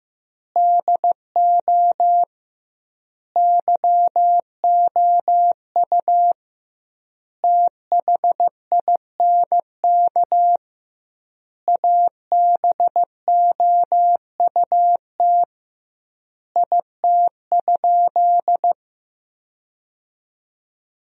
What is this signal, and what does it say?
Morse code, heard as a keyed tone, about 15 wpm, 700 Hz. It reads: DO YOU THINK ABOUT IT?
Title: 2x Word-Spacing - Sentences from Top 200 Words - 15wpm